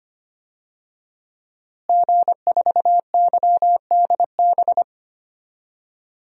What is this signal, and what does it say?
Morse code, heard as a keyed tone, about 25 wpm, 700 Hz. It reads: G4YDB
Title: DXCC-100 Call Signs - 5 Characters - Repeated 2x - 25wpm